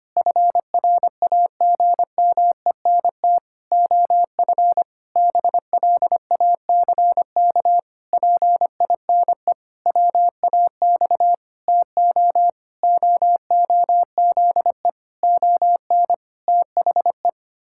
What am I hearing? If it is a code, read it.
Morse code, heard as a keyed tone, about 25 wpm, 700 Hz. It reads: FRAGMENT OF BLACK PINE WAX TO OOZE ON THE
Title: Pangram Challenge v2 - 25wpm